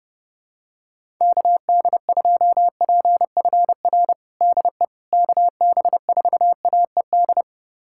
Morse code, heard as a keyed tone, about 32 wpm, 700 Hz. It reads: KD2PFR DE KB4AED
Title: USA Callsign DE Callsign - 6 Characters - 30wpm